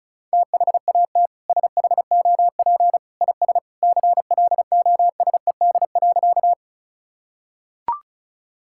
Morse code, heard as a keyed tone, about 34 wpm, 700 Hz. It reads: THAT SHOP IS CLOSED.
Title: Sentences from Top 1600 Words 35wpm